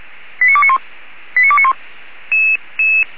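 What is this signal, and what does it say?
Five-tone selective call (ZVEI), heard as quick runs of five tones, tones of about 70 ms. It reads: 89291 89291
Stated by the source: Received on an SDR; ZVEI five-tone code 85997.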